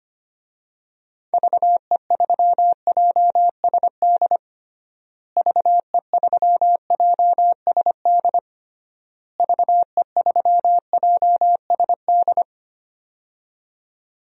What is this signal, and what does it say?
Morse code, heard as a keyed tone, about 25 wpm, 700 Hz. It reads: VE3JSD VE3JSD VE3JSD
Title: DXCC-100 Call Signs - 6 Characters - Repeated 3x - 25wpm